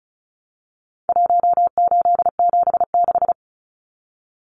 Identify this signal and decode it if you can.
Morse code, heard as a keyed tone, about 34 wpm, 700 Hz. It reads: 1876
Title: Numbers - Contest Style - 4-Digit 35wpm